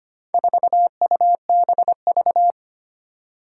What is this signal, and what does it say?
Morse code, heard as a keyed tone, about 25 wpm, 700 Hz. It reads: 4UBV
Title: Four Challenging Letter-Number Combinations 25wpm